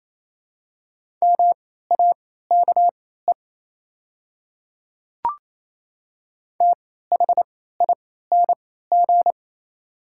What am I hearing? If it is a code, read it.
Morse code, heard as a keyed tone, about 27 wpm, spaced as for 16 wpm, 690 Hz. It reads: MAKE THING
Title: 3x Character Spacing - Top 100 Words - 28wpm